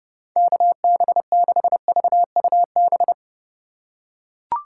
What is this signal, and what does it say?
Morse code, heard as a keyed tone, about 30 wpm, 700 Hz. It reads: KB6VUB